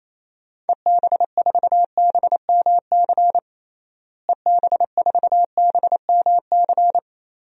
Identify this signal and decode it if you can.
Morse code, heard as a keyed tone, about 28 wpm, 700 Hz. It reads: EB4BMC EB4BMC
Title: DXCC-100 Call Signs - 6 Characters - Repeated 3x - 28wpm